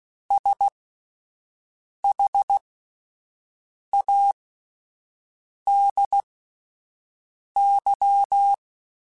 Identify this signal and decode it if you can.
Morse code, heard as a keyed tone, about 16 wpm, spaced as for 5 wpm, 800 Hz. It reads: SHADY